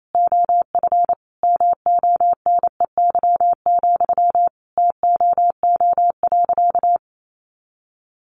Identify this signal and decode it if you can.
Morse code, heard as a keyed tone, about 28 wpm, 700 Hz. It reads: OF MONEY, TOO.
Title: Sentences from Top 900 Words 28wpm